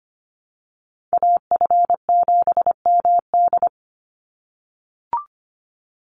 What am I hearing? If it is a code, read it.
Morse code, heard as a keyed tone, about 25 wpm, 700 Hz. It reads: AF7MD
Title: NA SOTA Chaser Callsigns 25wpm